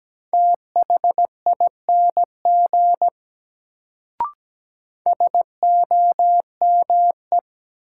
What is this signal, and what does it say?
Morse code, heard as a keyed tone, about 17 wpm, 700 Hz. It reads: THING SOME
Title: Sentences from Top 100 Words - New Word Drill - 17wpm